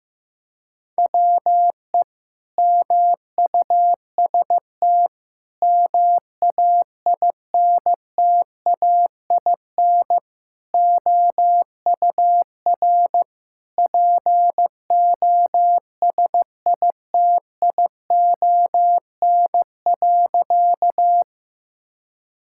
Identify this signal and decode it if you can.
Morse code, heard as a keyed tone, about 15 wpm, 700 Hz. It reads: WE MUST MAINTAIN OUR POSITION.
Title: Sentences from Top 1000 Words - Review Part 5 - 15wpm